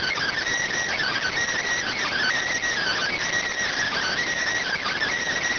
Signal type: Emergency Alert System (SAME) tones

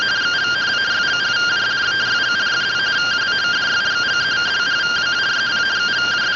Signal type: two-tone teleprinter signal

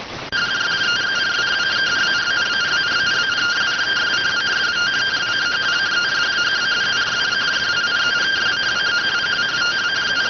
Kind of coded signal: two-tone teleprinter signal

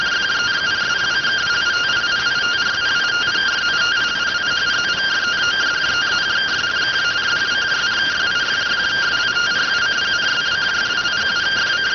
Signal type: two-tone teleprinter signal

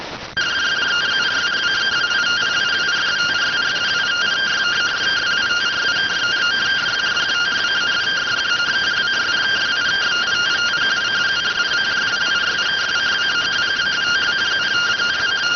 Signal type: two-tone teleprinter signal

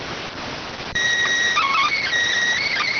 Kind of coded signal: SSTV (slow-scan television)

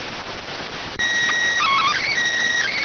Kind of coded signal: SSTV (slow-scan television)

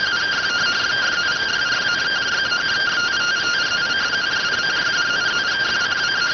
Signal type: two-tone teleprinter signal

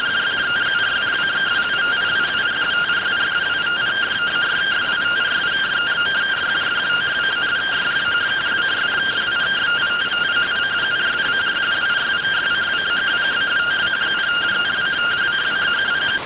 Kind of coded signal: two-tone teleprinter signal

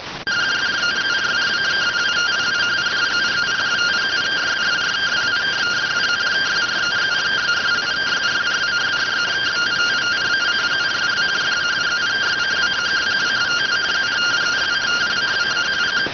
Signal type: two-tone teleprinter signal

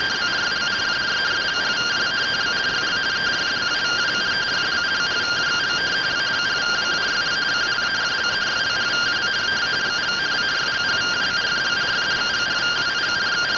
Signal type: two-tone teleprinter signal

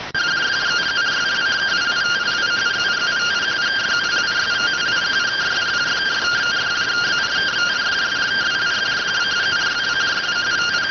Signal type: two-tone teleprinter signal

